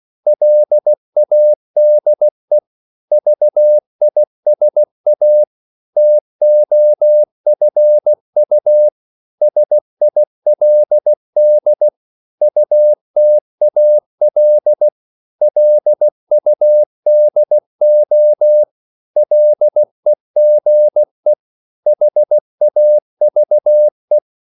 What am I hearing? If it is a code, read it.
Morse code, heard as a keyed tone, about 16 wpm, 600 Hz. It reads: LADE VISA TOFU SILD UTAL LUDO LEGE HAVE
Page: Korte danske ord 16wpm